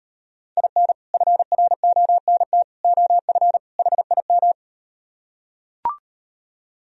Morse code, heard as a keyed tone, about 38 wpm, 700 Hz. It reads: IN FRONT OF HIM